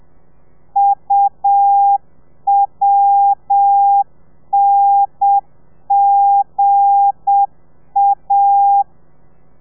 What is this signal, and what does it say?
Morse code, heard as a keyed tone, about 7 wpm, 800 Hz. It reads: UWNGA
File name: telegrafia / i2tfj / Lez8 30